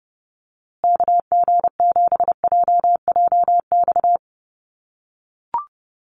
Morse code, heard as a keyed tone, about 30 wpm, 700 Hz. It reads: KG7JJX
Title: USA Call Signs Repeated 2x 30wpm